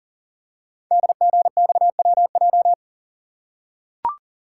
Morse code, heard as a keyed tone, about 42 wpm, 700 Hz. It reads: DGXWJ
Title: Code Groups - 5 Letters - 40wpm